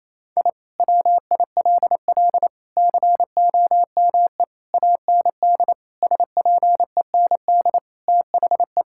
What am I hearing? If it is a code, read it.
Morse code, heard as a keyed tone, about 28 wpm, 700 Hz. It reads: I WILL COME AND SPEND THE
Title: Sentences from Top 1000 Words - Review Part 5 - 28wpm